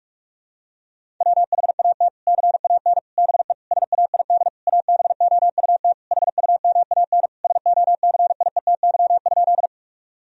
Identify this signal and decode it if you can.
Morse code, heard as a keyed tone, about 41 wpm, 700 Hz. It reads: WHAT CAN BE SAID ABOUT HUMAN SOCIETY?
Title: Sentences from Top 1000 Words - Review Part 4 - 45wpm